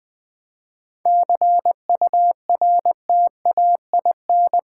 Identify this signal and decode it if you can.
Morse code, heard as a keyed tone, about 20 wpm, 700 Hz. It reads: CURTAIN
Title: 7 Letter Words - First 3 Letters Given - 20wpm